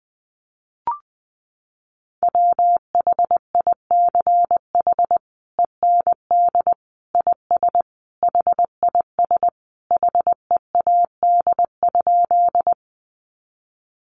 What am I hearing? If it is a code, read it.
Morse code, heard as a keyed tone, about 20 wpm, 700 Hz. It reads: WHICH END IS HIS HEAD?
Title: Sentences from Top 300 Words 20wpm